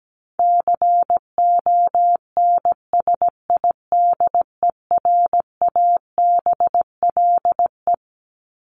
Morse code, heard as a keyed tone, about 17 wpm, 700 Hz. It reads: CONSIDERABLE
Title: LY Suffix - Top 100 Words with Base Words 17wpm